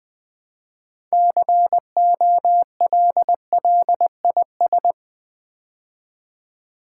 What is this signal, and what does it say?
Morse code, heard as a keyed tone, about 20 wpm, 700 Hz. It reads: COLLIS
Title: Build A Word - 3 Letters At A Time - 9-Letter Words - 20wpm